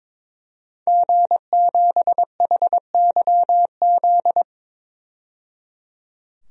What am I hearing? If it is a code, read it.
Morse code, heard as a keyed tone, about 22 wpm, 700 Hz. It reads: G7HYZ
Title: DXCC-100 Call Signs - 5 Characters - 22wpm